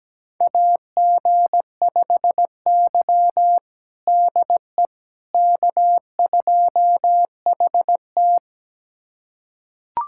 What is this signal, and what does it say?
Morse code, heard as a keyed tone, about 17 wpm, 700 Hz. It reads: AG5Y DE K2HT
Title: USA Callsign DE Callsign - 4 Characters - 17wpm